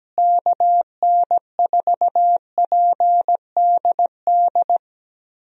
Morse code, heard as a keyed tone, about 17 wpm, 700 Hz. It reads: KN4PDD
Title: USA Call Signs - 6 Characters - Repeated 2x - 17wpm